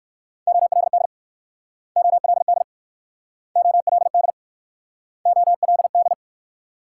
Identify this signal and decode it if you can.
Morse code, heard as a keyed tone, about 62 wpm, 690 Hz. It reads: OLD OLD OLD OLD